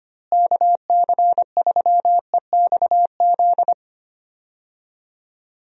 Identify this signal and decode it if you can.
Morse code, heard as a keyed tone, about 25 wpm, 700 Hz. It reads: KC3EXZ
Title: USA Call Signs - 6 Characters - Repeated 3x - 25wpm